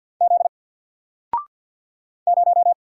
Morse code, heard as a keyed tone, about 50 wpm, 700 Hz. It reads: Z 0
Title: Single Letter-Number 50wpm